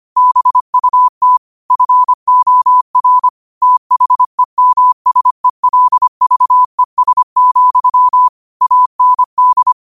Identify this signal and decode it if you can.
Morse code, heard as a keyed tone, about 25 wpm, 1000 Hz. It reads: BUT FOR THEMSELVES, AND